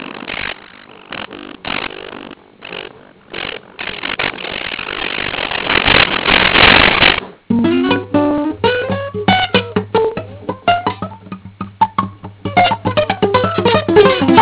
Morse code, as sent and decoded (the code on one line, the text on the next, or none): none